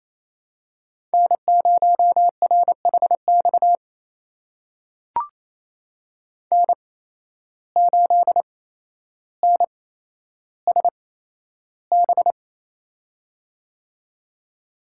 Morse code, heard as a keyed tone, about 28 wpm, spaced as for 12 wpm, 700 Hz. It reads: N0RHX N8NSB
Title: USA Call Signs - 5 Characters - 8x Character Spacing - 28wpm